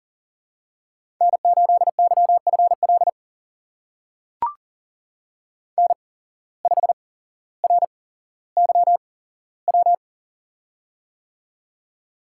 Morse code, heard as a keyed tone, about 38 wpm, spaced as for 17 wpm, 700 Hz. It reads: N8YFL N5RYW
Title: USA Call Signs - 8x Character Spacing - 40wpm